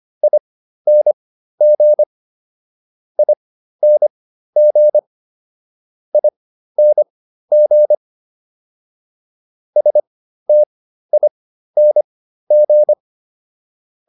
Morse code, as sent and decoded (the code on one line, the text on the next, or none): .. -. --. / .. -. --. / .. -. --. / ... - .. -. --.
ING ING ING STING